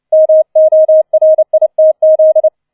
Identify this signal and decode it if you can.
Morse code, heard as a keyed tone, about 29 wpm, 630 Hz. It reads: MORITZ